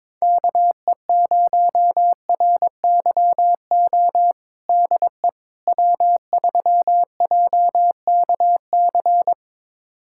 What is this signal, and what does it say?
Morse code, heard as a keyed tone, about 22 wpm, 700 Hz. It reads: KE0RYO DE W3JKC
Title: USA Callsign DE Callsign 22wpm